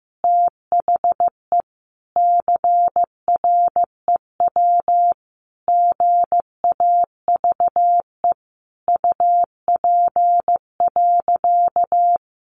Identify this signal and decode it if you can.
Morse code, heard as a keyed tone, about 14 wpm, 700 Hz. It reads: THE CREW GAVE UP.
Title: Sentences from Top 1500 Words 15wpm